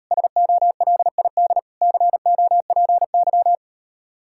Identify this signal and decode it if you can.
Morse code, heard as a keyed tone, about 39 wpm, 700 Hz. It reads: SOLID COPY